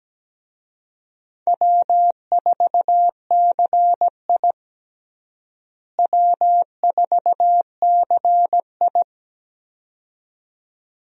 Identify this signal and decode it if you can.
Morse code, heard as a keyed tone, about 17 wpm, 700 Hz. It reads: W4CI W4CI
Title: USA Call Signs - 4 Characters - Repeated 2x - 17wpm